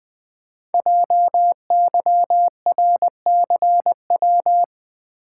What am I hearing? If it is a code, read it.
Morse code, heard as a keyed tone, about 20 wpm, 700 Hz. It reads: JYRCW
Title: Code Groups - 5 Letter:Numbers - 20wpm